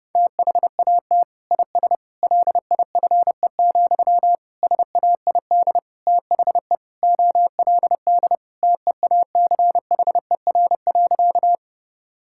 Morse code, heard as a keyed tone, about 30 wpm, 700 Hz. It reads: THAT IS LIFE, SAID THE OLD TEACHER.
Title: Sentences from Top 1000 Words - Review Part 6 - 30wpm